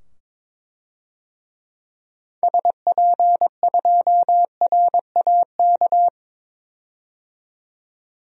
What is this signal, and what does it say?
Morse code, heard as a keyed tone, about 22 wpm, 700 Hz. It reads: SP2RAK
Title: DXCC-100 Call Signs - 6 Characters - 22wpm